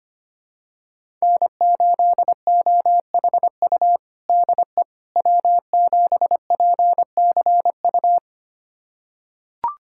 Morse code, heard as a keyed tone, about 25 wpm, 700 Hz. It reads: N8OHU DE W7PCU
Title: USA Callsign DE Callsign - 5 Characters - 25wpm